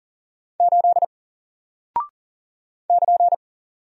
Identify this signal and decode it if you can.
Morse code, heard as a keyed tone, about 40 wpm, 700 Hz. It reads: 8 <KN>
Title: Single Character (Letter:Number:Punctuation:Prosign) Alt 40wpm